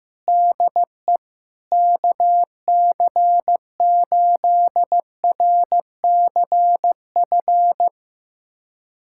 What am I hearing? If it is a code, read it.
Morse code, heard as a keyed tone, about 15 wpm, 700 Hz. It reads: DE KC8RCF
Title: USA Callsign DE Callsign - 6 Characters - 15wpm